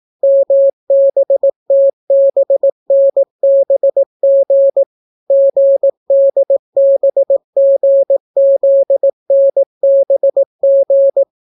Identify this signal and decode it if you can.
Morse code, heard as a keyed tone, about 18 wpm, 550 Hz. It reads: MBTBNBG GDBGZNBG